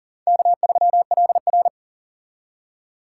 Morse code, heard as a keyed tone, about 40 wpm, 700 Hz. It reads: K3LR
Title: K1USN SST QSO Practice - 40wpm